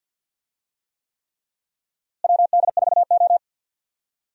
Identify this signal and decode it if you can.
Morse code, heard as a keyed tone, about 53 wpm, 700 Hz. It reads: WD4O
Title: USA Call Signs - 4 Characters - Repeated 3x - 50wpm